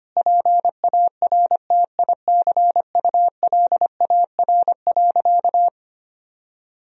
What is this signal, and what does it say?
Morse code, heard as a keyed tone, about 25 wpm, 700 Hz. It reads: PARTICULAR.
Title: Sentences from Top 1000 Words - Review Part 4 - 25wpm